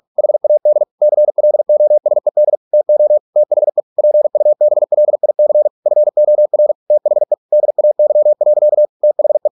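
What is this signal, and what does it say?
Morse code, heard as a keyed tone, about 46 wpm, 600 Hz. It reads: HAD CLOSED TO THE PUBLIC FOR THE DAY. THE
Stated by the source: Morse practice at 45 wpm, the novel Never by Ken Follett in English, from CW med Gnister.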